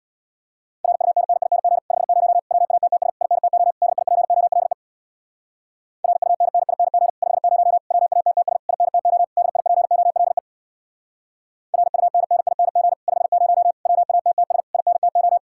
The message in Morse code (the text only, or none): RUNNING 50 WATTS INTO DIPOLE RUNNING 50 WATTS INTO DIPOLE RUNNING 50 WATTS INTO